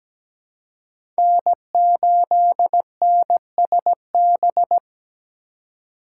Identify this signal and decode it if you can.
Morse code, heard as a keyed tone, about 17 wpm, 700 Hz. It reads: N8NSB
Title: USA Call Signs - 5 Characters - 3x Character Spacing - 17wpm